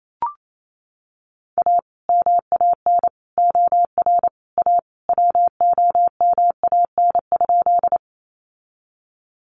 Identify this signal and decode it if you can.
Morse code, heard as a keyed tone, about 28 wpm, 700 Hz. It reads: A MAN OR A WOMAN?
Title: Sentences from Top 1100 Words 28wpm